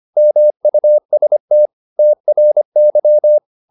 Morse code, heard as a keyed tone, about 25 wpm, 600 Hz. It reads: MUST TRY